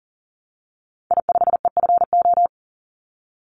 Morse code, heard as a keyed tone, about 40 wpm, 700 Hz. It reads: I5EFO
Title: Call Signs Repeated 3x - 40wpm